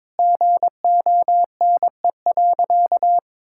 Morse code, heard as a keyed tone, about 22 wpm, 700 Hz. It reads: GONE.
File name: practice / 22wpm / Sentences with 4-Word Phrases - 22wpm